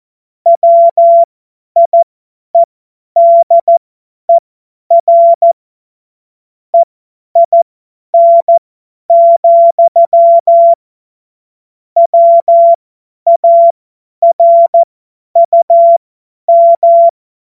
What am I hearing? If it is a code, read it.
Morse code, heard as a keyed tone, about 14 wpm, spaced as for 10 wpm, 680 Hz. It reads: WIEDER EIN, WARUM
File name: Morsekurs / Witze / Witz 18. April 2023 50 F70